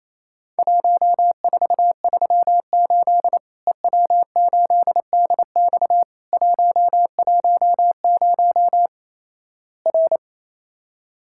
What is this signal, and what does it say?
Morse code, heard as a keyed tone, about 28 wpm, 700 Hz. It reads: 1438 EW8DX 110 R
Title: Worked All Europe DX Contest - QTC Exchange - 28wpm